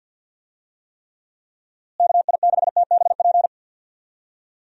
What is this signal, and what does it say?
Morse code, heard as a keyed tone, about 50 wpm, 700 Hz. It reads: KI6TBP
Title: USA Call Signs - 6 Characters - Repeated 2x - 50wpm